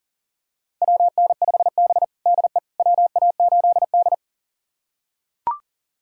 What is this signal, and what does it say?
Morse code, heard as a keyed tone, about 40 wpm, 700 Hz. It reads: WN5B DE WA8D
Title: USA Callsign DE Callsign - 4 Characters - 40wpm